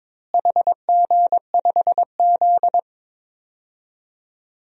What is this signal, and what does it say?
Morse code, heard as a keyed tone, about 22 wpm, 700 Hz. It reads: HG5Z
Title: DXCC-100 Call Signs - 4 Characters - Repeated 2x - 22wpm